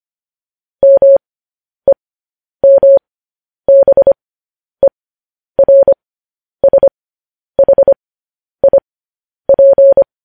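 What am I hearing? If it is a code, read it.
Morse code, heard as a keyed tone, about 25 wpm, spaced as for 10 wpm, 550 Hz. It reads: MEMBERSHIP
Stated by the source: Morse code practice sending news headlines at 10 wpm.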